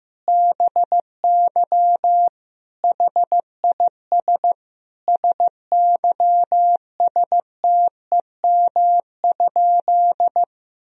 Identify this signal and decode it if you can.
Morse code, heard as a keyed tone, about 15 wpm, 700 Hz. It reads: BY HIS SYSTEM?
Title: Sentences from Top 1000 Words - Review Part 3 - 15wpm